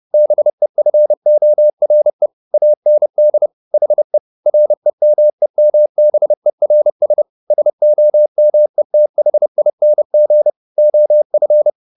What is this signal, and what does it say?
Morse code, heard as a keyed tone, about 30 wpm, 600 Hz. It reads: BEFORE AND HE REMEMBERS SOMETHING OF
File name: the old man and the sea / the old man and the sea 1 30wpm